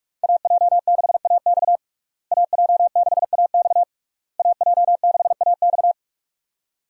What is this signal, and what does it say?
Morse code, heard as a keyed tone, about 46 wpm, 700 Hz. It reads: AJ6AX AJ6AX AJ6AX
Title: USA Call Signs - 5 Characters - Repeated 3x - 45wpm